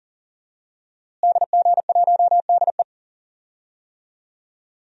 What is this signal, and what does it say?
Morse code, heard as a keyed tone, about 40 wpm, 700 Hz. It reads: DG1DE
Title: DXCC-100 Call Signs - 5 Characters - Repeated 3x - 40wpm